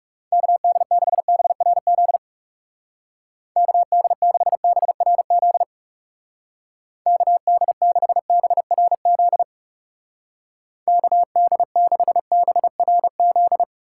Morse code, heard as a keyed tone, about 36 wpm, 700 Hz. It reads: KD6BRZ KD6BRZ KD6BRZ KD6BRZ
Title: Speed Racing - USA 2x3 Call Signs 30wpm